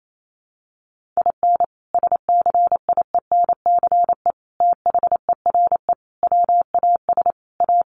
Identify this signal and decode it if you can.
Morse code, heard as a keyed tone, about 28 wpm, 700 Hz. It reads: IN SCIENCE THERE WAS A
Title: Sentences from Top 1400 Words 28wpm